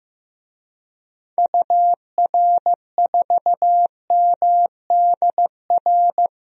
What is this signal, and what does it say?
Morse code, heard as a keyed tone, about 15 wpm, 700 Hz. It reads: UR4MDR
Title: DXCC-100 Call Signs - 6 Characters - Repeated 3x - 15wpm